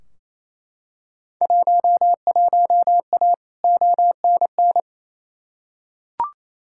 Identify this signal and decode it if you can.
Morse code, heard as a keyed tone, about 28 wpm, 700 Hz. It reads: 11A ONN
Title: Field Day Exchanges Encore - 28wpm